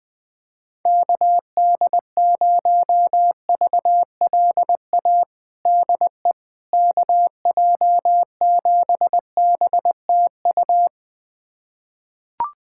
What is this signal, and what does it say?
Morse code, heard as a keyed tone, about 21 wpm, 700 Hz. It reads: KD0VLA DE KJ7BTU
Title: USA Callsign DE Callsign - 6 Characters - 20wpm